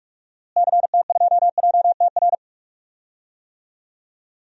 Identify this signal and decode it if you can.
Morse code, heard as a keyed tone, about 41 wpm, 700 Hz. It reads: CT2JTR
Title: DXCC-100 Call Signs - 6 Characters - Repeated 3x - 45wpm